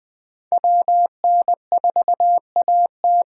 Morse code, heard as a keyed tone, about 21 wpm, 700 Hz. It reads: WN4AT
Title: 2023 SOTA Chaser Call Signs - 20wpm